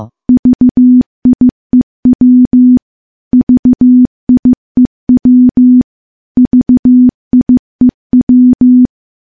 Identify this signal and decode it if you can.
Morse code, heard as a keyed tone, about 15 wpm, 250 Hz. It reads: VIEW VIEW VIEW
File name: morse / morse-training-15WPM